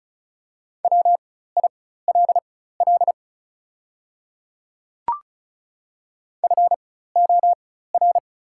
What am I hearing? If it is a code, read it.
Morse code, heard as a keyed tone, about 34 wpm, spaced as for 16 wpm, 700 Hz. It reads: WILL FOR
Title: 4x Character Spacing - Top 200 Words - 35wpm